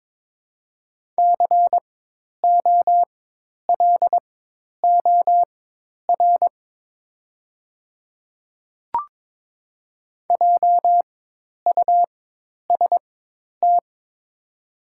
Morse code, heard as a keyed tone, about 22 wpm, spaced as for 10 wpm, 700 Hz. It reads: COLOR JUST
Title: 4x Character Spacing - Top 300 Words - 22wpm